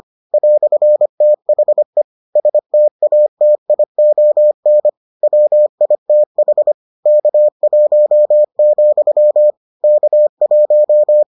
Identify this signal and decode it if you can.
Morse code, heard as a keyed tone, about 25 wpm, 600 Hz. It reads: "THE STATION WITH K1, K1